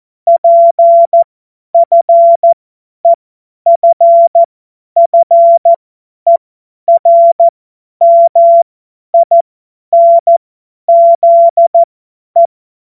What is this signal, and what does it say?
Morse code, heard as a keyed tone, about 14 wpm, spaced as for 10 wpm, 680 Hz. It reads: PFEFFERMINZE
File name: Morsekurs / Witze / Witz 30 Januar 2024 50 F70